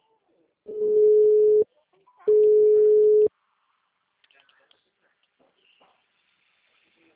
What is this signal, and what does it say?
Ringback tone (Europe, 425 Hz), with speech.